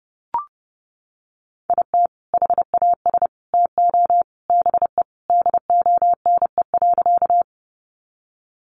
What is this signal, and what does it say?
Morse code, heard as a keyed tone, about 30 wpm, 700 Hz. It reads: IT HAS TO BE DONE.
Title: Sentences from Top 500 Words 30wpm